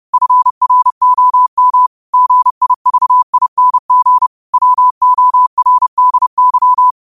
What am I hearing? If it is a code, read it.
Morse code, heard as a keyed tone, about 30 wpm, 1000 Hz. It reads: FROM GIVING WORDY
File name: qotdfiles / 30wpm